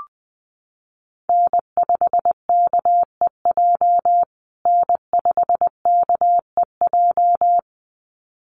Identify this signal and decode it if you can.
Morse code, heard as a keyed tone, about 20 wpm, 700 Hz. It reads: N5KEJ N5KEJ
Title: USA Call Signs Repeated 2x 20wpm